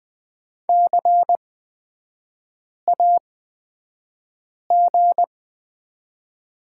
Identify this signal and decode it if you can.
Morse code, heard as a keyed tone, about 20 wpm, spaced as for 5 wpm, 700 Hz. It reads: CAG
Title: 243 Lesson 39e - Letter z - Three Characters 20wpm